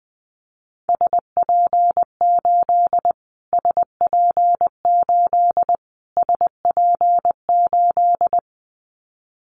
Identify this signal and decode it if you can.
Morse code, heard as a keyed tone, about 20 wpm, 700 Hz. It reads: SP8 SP8 SP8
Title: Callsign Prefixes for Top 25 DXCC - Repeated 3x - 20wpm